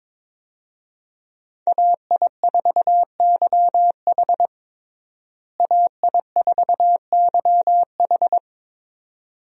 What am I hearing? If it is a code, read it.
Morse code, heard as a keyed tone, about 22 wpm, 700 Hz. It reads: AI4YH AI4YH
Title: USA Call Signs - 5 Characters - Repeated 2x - 22wpm